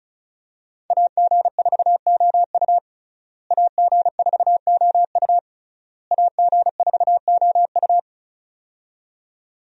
Morse code, heard as a keyed tone, about 36 wpm, 700 Hz. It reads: AG4OU AG4OU AG4OU
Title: USA Call Signs - 5 Characters - Repeated 3x - 35wpm